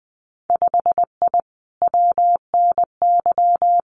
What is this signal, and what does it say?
Morse code, heard as a keyed tone, about 20 wpm, 700 Hz. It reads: 5I WNY
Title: Winter Field Day Exchanges Encore 20wpm